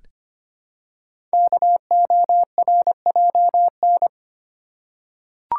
Morse code, heard as a keyed tone, about 25 wpm, 700 Hz. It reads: KORJN
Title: Code Groups - 5 Letters - 25wpm